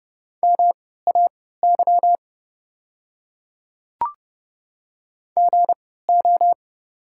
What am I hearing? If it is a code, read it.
Morse code, heard as a keyed tone, about 29 wpm, spaced as for 17 wpm, 700 Hz. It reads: MAY GO
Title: 3x Character Spacing - Top 100 Words - 30wpm